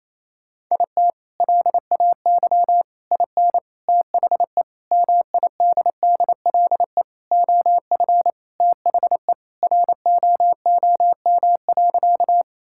Morse code, heard as a keyed tone, about 28 wpm, 700 Hz. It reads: IT LAY IN THE MIDDLE OF THE ROOM.
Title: Sentences from Top 1000 Words - Review Part 6 - 28wpm